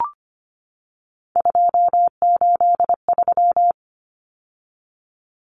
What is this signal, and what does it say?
Morse code, heard as a keyed tone, about 25 wpm, 700 Hz. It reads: 283
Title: Common Cut Numbers - 3-Digit 25wpm